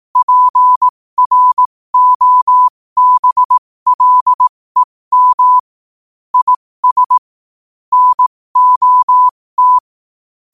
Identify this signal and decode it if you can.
Morse code, heard as a keyed tone, about 18 wpm, spaced as for 15 wpm, 1000 Hz. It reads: PROBLEM IS NOT
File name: qotdfiles / 15wpm